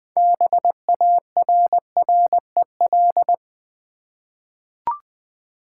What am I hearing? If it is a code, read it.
Morse code, heard as a keyed tone, about 20 wpm, 700 Hz. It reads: BARREL
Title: 6 Letter Words - First 3 Letters Given - 20wpm